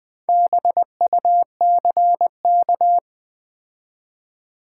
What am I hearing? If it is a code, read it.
Morse code, heard as a keyed tone, about 20 wpm, 700 Hz. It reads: BUCK
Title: Build A Word - 2 Letters At A Time - 4-Letter Words - 20wpm